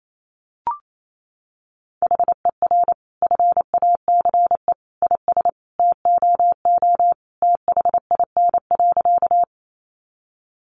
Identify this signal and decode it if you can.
Morse code, heard as a keyed tone, about 28 wpm, 700 Hz. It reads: HER FACE IS TOO THIN.
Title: Sentences from Top 1200 Words 28wpm